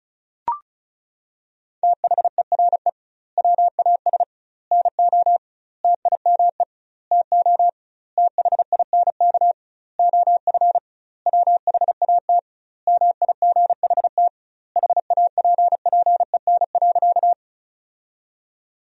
Morse code, heard as a keyed tone, about 34 wpm, 700 Hz. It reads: THERE WAS NO TIME TO THINK OF WHAT MIGHT HAPPEN.
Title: 2x Word-Spacing - Sentences from Top 400 Words - 35wpm